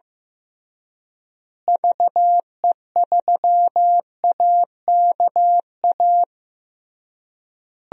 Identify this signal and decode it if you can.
Morse code, heard as a keyed tone, about 15 wpm, 700 Hz. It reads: VE3AKA
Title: DXCC-100 Call Signs - 6 Characters - Repeated 3x - 15wpm